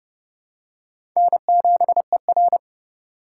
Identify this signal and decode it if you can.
Morse code, heard as a keyed tone, about 31 wpm, 700 Hz. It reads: N7ER
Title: USA Call Signs - 4 Characters - 6x Character Spacing - 30wpm